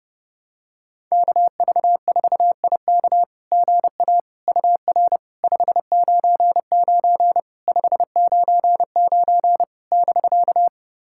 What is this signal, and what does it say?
Morse code, heard as a keyed tone, about 30 wpm, 690 Hz. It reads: KV4IK GA UR 599 599 <BK>
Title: SOTA QSO Practice 30wpm